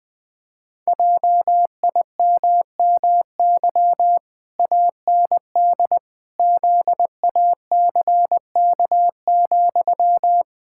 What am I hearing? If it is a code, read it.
Morse code, heard as a keyed tone, about 20 wpm, 700 Hz. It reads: JIMMY AND ZACK,
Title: Pangram Challenge v2 - 20wpm